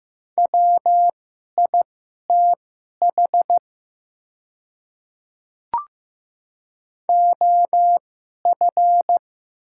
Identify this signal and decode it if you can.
Morse code, heard as a keyed tone, about 15 wpm, spaced as for 11 wpm, 700 Hz. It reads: WITH OF
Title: 2x Character Spacing - Top 100 Words - 15wpm